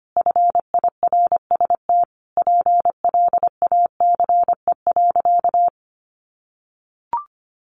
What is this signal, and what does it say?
Morse code, heard as a keyed tone, about 25 wpm, 700 Hz. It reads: FIRST PLACE.
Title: Additional Easy Phrases 25wpm